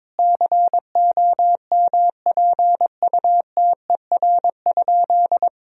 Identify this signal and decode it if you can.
Morse code, heard as a keyed tone, about 22 wpm, 700 Hz. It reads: COMPUTER?
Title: Sentences from Top 1000 Words - Review Part 3 - 22wpm